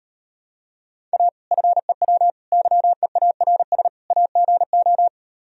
Morse code, heard as a keyed tone, about 39 wpm, 700 Hz. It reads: A FEW YEARS AGO